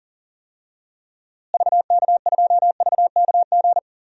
Morse code, heard as a keyed tone, about 40 wpm, 700 Hz. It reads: VK2VKG
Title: DXCC-100 Call Signs - 6 Characters - Repeated 2x - 40wpm